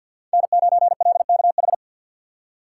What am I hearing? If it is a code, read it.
Morse code, heard as a keyed tone, about 51 wpm, 700 Hz. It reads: N9LKH
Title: USA Call Signs - 5 Characters - 2x Character Spacing - 50wpm